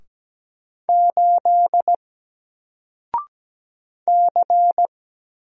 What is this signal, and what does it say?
Morse code, heard as a keyed tone, about 17 wpm, 700 Hz. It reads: 8 C
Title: Single Character (Letter:Number:Punctuation:Prosign) 17wpm